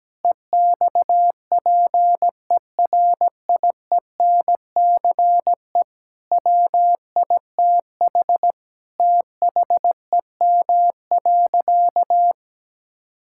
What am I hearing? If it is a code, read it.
Morse code, heard as a keyed tone, about 17 wpm, 700 Hz. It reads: EXPERIENCE WITH THEM.
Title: Sentences from Top 1000 Words - Review Part 6 - 17wpm